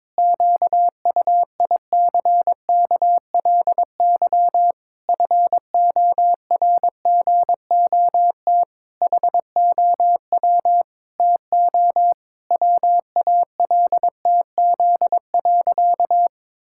Morse code, heard as a keyed tone, about 22 wpm, 700 Hz. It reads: QUICKLY FORGOT HOW TO WALTZ.
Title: Pangram Challenge v2 - 22wpm